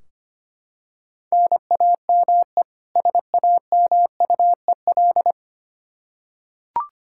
Morse code, heard as a keyed tone, about 25 wpm, 700 Hz. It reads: NAME SAMUEL
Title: Practice QSO - Exchange 1 - Beginning - 25wpm